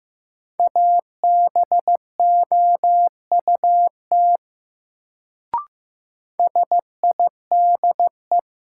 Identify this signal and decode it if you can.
Morse code, heard as a keyed tone, about 15 wpm, 700 Hz. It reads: ABOUT SIDE
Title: Sets of 2 Words - New Word Drill - 15wpm